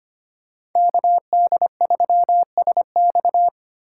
Morse code, heard as a keyed tone, about 25 wpm, 700 Hz. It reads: KD3SX
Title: USA Call Signs - 5 Characters - 2x Character Spacing - 25wpm